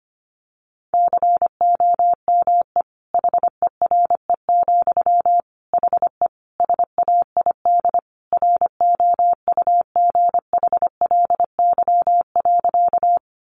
Morse code, heard as a keyed tone, about 25 wpm, 700 Hz. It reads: COME HERE, HE SAID ROUGHLY.
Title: Sentences from Top 1500 Words 25wpm